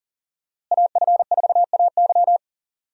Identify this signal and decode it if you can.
Morse code, heard as a keyed tone, about 40 wpm, 700 Hz. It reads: AF4AY
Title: USA Call Signs - 5 Characters - Repeated 3x - 40wpm